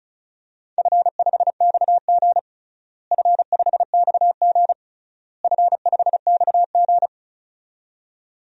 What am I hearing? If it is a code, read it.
Morse code, heard as a keyed tone, about 36 wpm, 700 Hz. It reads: F5XG F5XG F5XG
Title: DXCC-100 Call Signs - 4 Characters - Repeated 3x - 35wpm